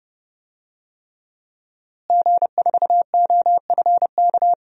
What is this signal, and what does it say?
Morse code, heard as a keyed tone, about 30 wpm, 700 Hz. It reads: G4OFK
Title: DXCC-100 Call Signs - 5 Characters - Repeated 3x - 30wpm